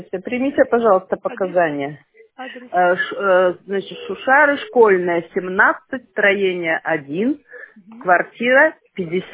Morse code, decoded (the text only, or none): none